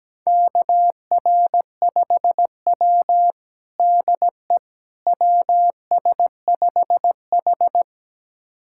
Morse code, heard as a keyed tone, about 17 wpm, 700 Hz. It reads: KR5W DE WS5H
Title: USA Callsign DE Callsign - 4 Characters - 17wpm